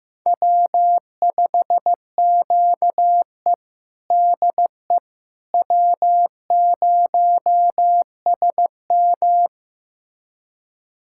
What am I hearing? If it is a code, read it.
Morse code, heard as a keyed tone, about 15 wpm, 700 Hz. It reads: W5QE DE W0SM
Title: USA Callsign DE Callsign - 4 Characters - 15wpm